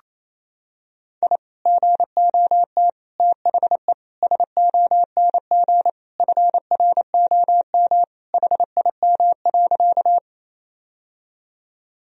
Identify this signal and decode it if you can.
Morse code, heard as a keyed tone, about 28 wpm, 700 Hz. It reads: I GOT THE SONG FROM HIM.
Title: Sentences from Top 1000 Words - Review Part 2 - 28wpm